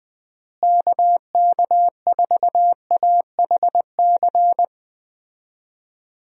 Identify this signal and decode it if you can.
Morse code, heard as a keyed tone, about 20 wpm, 700 Hz. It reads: KK4AHC
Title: USA Call Signs - 6 Characters - Repeated 3x - 20wpm